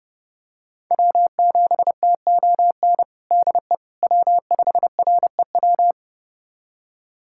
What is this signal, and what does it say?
Morse code, heard as a keyed tone, about 30 wpm, 700 Hz. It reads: W7TON DE W5REW
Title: USA Callsign DE Callsign - 5 Characters - 30wpm